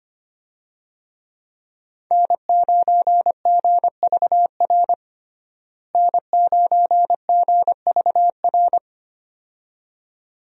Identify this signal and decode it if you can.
Morse code, heard as a keyed tone, about 25 wpm, 700 Hz. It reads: N9GVR N9GVR
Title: USA Call Signs - 5 Characters - Repeated 2x - 25wpm